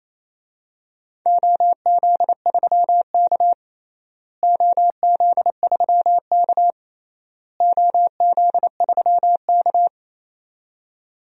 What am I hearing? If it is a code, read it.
Morse code, heard as a keyed tone, about 28 wpm, 700 Hz. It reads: OZ3K OZ3K OZ3K
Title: DXCC-100 Call Signs - 4 Characters - Repeated 3x - 28wpm